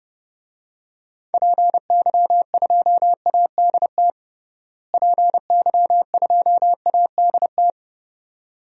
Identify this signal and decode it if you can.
Morse code, heard as a keyed tone, about 31 wpm, 700 Hz. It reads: PY2ADT PY2ADT
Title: DXCC-100 Call Signs - 6 Characters - Repeated 2x - 30wpm